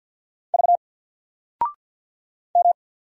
Morse code, heard as a keyed tone, about 50 wpm, 700 Hz. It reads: V M
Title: Single Letter-Number 50wpm